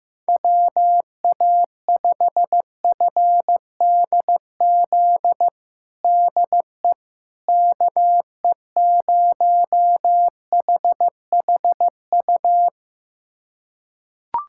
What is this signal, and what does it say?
Morse code, heard as a keyed tone, about 15 wpm, 700 Hz. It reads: WA5FDZ DE KE0HHU
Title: USA Callsign DE Callsign 15wpm